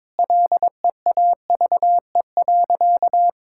Morse code, heard as a keyed tone, about 22 wpm, 700 Hz. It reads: LEAVE.
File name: practice / 22wpm / Sentences with 4-Word Phrases - 22wpm